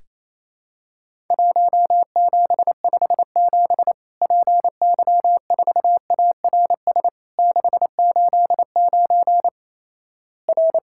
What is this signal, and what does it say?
Morse code, heard as a keyed tone, about 27 wpm, 700 Hz. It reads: 1757 PY4ARS 689 R
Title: Worked All Europe DX Contest - QTC Exchange - 28wpm